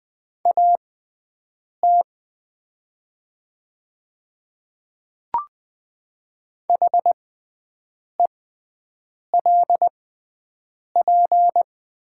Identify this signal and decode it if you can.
Morse code, heard as a keyed tone, about 20 wpm, spaced as for 7 wpm, 700 Hz. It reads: AT HELP
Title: 6x Character Spacing - Top 200 Words - 20wpm